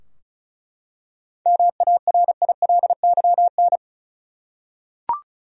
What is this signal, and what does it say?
Morse code, heard as a keyed tone, about 36 wpm, 700 Hz. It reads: MARILYN
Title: USA Common Names - 35wpm